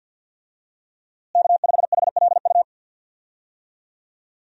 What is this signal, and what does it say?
Morse code, heard as a keyed tone, about 48 wpm, 700 Hz. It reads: K5HLU
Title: USA Call Signs - 5 Characters - Repeated 2x - 50wpm